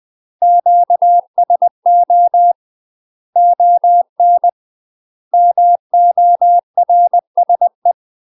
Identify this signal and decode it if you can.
Morse code, heard as a keyed tone, about 20 wpm, 700 Hz. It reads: QSO ON MORSE
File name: UK MASTER QSOs TRAINING / 20WPM / UK MASTER QSOs 0045